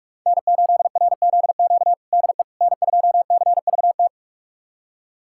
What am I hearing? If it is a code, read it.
Morse code, heard as a keyed tone, about 47 wpm, 700 Hz. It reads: N8RZQ DE N2CVT
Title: USA Callsign DE Callsign - 5 Characters - 45wpm